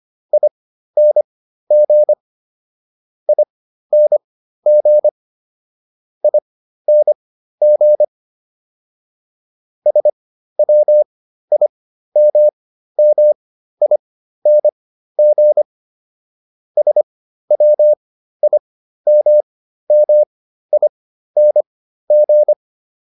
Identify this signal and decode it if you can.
Morse code, heard as a keyed tone, about 25 wpm, spaced as for 13 wpm, 600 Hz. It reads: ING ING ING SWIMMING SWIMMING